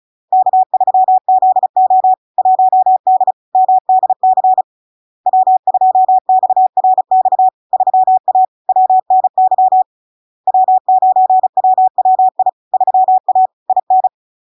Morse code, heard as a keyed tone, about 36 wpm, 750 Hz. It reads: K3ZO 1D MDC W2XRX 3A WNY W9WWI 3A IN
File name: LARK-FD-2022-Log-in-CW-35wpm-750-Hz